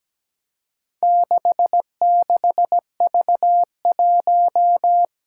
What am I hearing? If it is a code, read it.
Morse code, heard as a keyed tone, about 17 wpm, 700 Hz. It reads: 66V1
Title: Four Challenging Letter-Number Combinations 17wpm